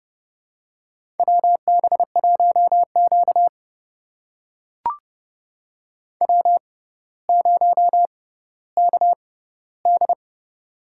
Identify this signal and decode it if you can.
Morse code, heard as a keyed tone, about 30 wpm, spaced as for 15 wpm, 700 Hz. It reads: WB1Q W0KD
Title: USA Call Signs - 6x Character Spacing - 30wpm